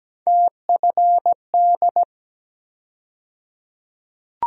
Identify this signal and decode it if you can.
Morse code, heard as a keyed tone, about 17 wpm, 700 Hz. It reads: TFD
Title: USA Call Sign Postfixes - 3 Letters 17wpm